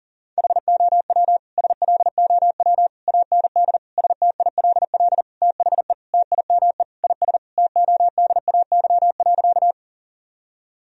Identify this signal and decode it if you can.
Morse code, heard as a keyed tone, about 40 wpm, 700 Hz. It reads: HOW SLOW AND STILL THE TIME IS TODAY.
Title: Sentences from Top 1000 Words - Review Part 2 - 40wpm